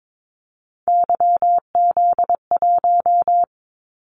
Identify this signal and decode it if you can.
Morse code, heard as a keyed tone, about 22 wpm, 700 Hz. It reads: YZ1
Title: Callsign Prefixes for Top 25 DXCC - 22wpm